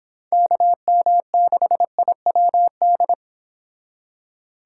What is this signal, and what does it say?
Morse code, heard as a keyed tone, about 25 wpm, 700 Hz. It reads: KM6IWD